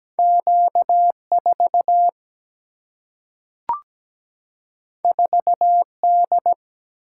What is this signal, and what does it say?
Morse code, heard as a keyed tone, about 17 wpm, 700 Hz. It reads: Q 4 4 D
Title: Single-Letter Single-Number 17wpm